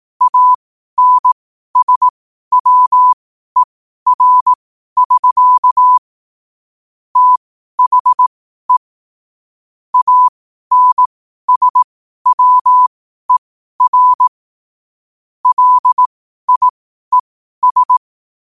Morse code, heard as a keyed tone, about 18 wpm, spaced as for 12 wpm, 1000 Hz. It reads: ANSWER<SK> THE ANSWER LIES
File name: qotdfiles / 12wpm